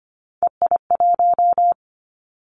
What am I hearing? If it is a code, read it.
Morse code, heard as a keyed tone, about 25 wpm, 700 Hz. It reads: EI1
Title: Callsign Prefixes for Top 50 DXCC - 25wpm